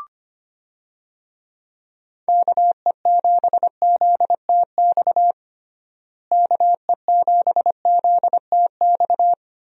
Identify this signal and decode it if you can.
Morse code, heard as a keyed tone, about 25 wpm, 700 Hz. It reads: KE7ZTX KE7ZTX
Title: USA Call Signs - 6 Characters - Repeated 3x - 25wpm